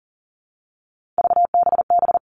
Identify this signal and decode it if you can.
Morse code, heard as a keyed tone, about 38 wpm, 700 Hz. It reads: VBB
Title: Three Challenging Letter-Number Combinations 40wpm